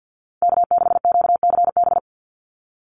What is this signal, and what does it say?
Morse code, heard as a keyed tone, about 50 wpm, 700 Hz. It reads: K6QCB
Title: NA SOTA Chaser Callsigns 50wpm